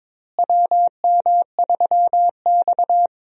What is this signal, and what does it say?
Morse code, heard as a keyed tone, about 22 wpm, 700 Hz. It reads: WM3X
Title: USA Call Signs - 4 Characters - Repeated 3x - 22wpm